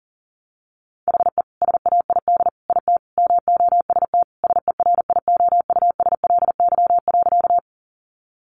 Morse code, heard as a keyed tone, about 40 wpm, 700 Hz. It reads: HE SAID IT MOST SERIOUSLY.
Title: Sentences from Top 1200 Words 40wpm